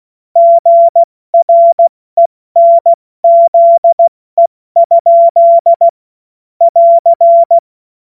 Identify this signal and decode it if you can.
Morse code, heard as a keyed tone, about 16 wpm, 680 Hz. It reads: GRENZE? <AR>
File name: Morsekurs / Witze / Witz 28. Dezember 2021, 70 F80